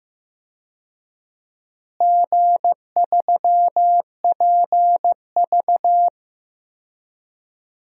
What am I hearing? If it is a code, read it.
Morse code, heard as a keyed tone, about 15 wpm, 700 Hz. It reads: G3PV
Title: DXCC-100 Call Signs - 4 Characters - Repeated 3x - 15wpm